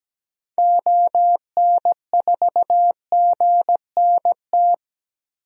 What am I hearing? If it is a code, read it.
Morse code, heard as a keyed tone, about 17 wpm, 700 Hz. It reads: ON4GNT
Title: DXCC-100 Call Signs - 6 Characters - 17wpm